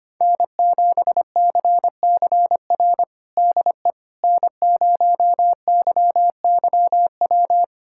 Morse code, heard as a keyed tone, about 25 wpm, 700 Hz. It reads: N7CCR DE N0YYW
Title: USA Callsign DE Callsign - 5 Characters - 25wpm